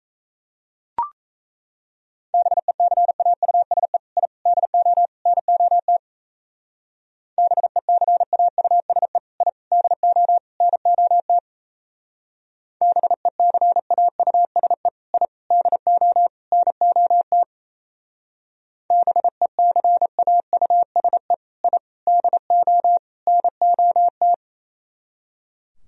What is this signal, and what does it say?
Morse code, heard as a keyed tone, about 36 wpm, 700 Hz. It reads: BECAUSE I DO NOT BECAUSE I DO NOT BECAUSE I DO NOT BECAUSE I DO NOT